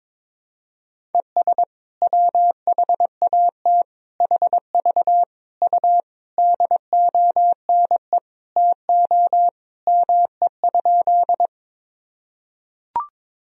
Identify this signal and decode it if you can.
Morse code, heard as a keyed tone, about 22 wpm, 700 Hz. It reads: ES WHAT HV U DONE TO ME?
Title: Sentences from Top 500 Words and Abbreviations 22wpm